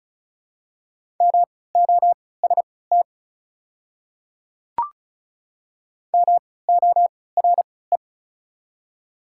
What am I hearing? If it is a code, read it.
Morse code, heard as a keyed tone, about 36 wpm, spaced as for 20 wpm, 690 Hz. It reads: MOST MORE
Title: 3x Character Spacing - Top 100 Words - 35wpm